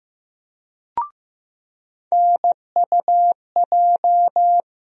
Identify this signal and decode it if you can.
Morse code, heard as a keyed tone, about 15 wpm, 700 Hz. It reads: NUJ
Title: USA Call Sign Postfixes - 3 Letters 15wpm